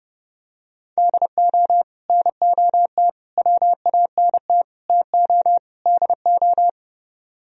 Morse code, heard as a keyed tone, about 30 wpm, 700 Hz. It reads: DO NOT WANT TO DO